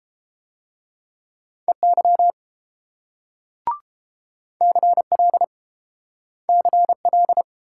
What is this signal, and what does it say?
Morse code, heard as a keyed tone, about 30 wpm, 700 Hz. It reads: EY CL CL